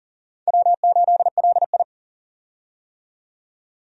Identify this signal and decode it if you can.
Morse code, heard as a keyed tone, about 37 wpm, 700 Hz. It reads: W8LI
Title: USA Call Signs - 4 Characters - 40wpm